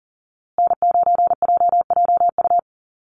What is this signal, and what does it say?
Morse code, heard as a keyed tone, about 40 wpm, 700 Hz. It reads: N9JJU
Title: USA Call Signs Repeated 3x 40wpm